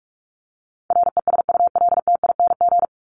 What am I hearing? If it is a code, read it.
Morse code, heard as a keyed tone, about 44 wpm, 700 Hz. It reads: RESULTING
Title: ING Suffix - Top 100 Words with Base Words 45wpm